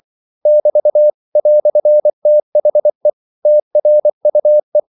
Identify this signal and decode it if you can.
Morse code, heard as a keyed tone, about 24 wpm, 600 Hz. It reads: <BT> "THE TRUE
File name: CW / Practice / EnglishQuotations0224